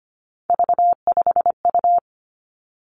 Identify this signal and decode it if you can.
Morse code, heard as a keyed tone, about 25 wpm, 700 Hz. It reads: V5U
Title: Three Challenging Letter-Number Combinations 25wpm